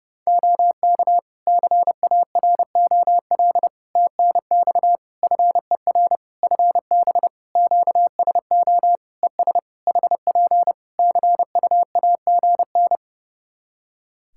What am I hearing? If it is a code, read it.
Morse code, heard as a keyed tone, about 30 wpm, 700 Hz. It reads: OK CAROL TNX FER FB QSO ES HP CUAGN
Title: K1USN SST QSO Practice 30wpm